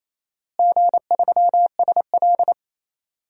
Morse code, heard as a keyed tone, about 28 wpm, 700 Hz. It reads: G3SL
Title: DXCC-100 Call Signs - 4 Characters - Repeated 2x - 28wpm